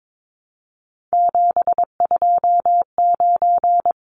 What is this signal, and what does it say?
Morse code, heard as a keyed tone, about 22 wpm, 700 Hz. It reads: 729
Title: Numbers - 3-Digit 22wpm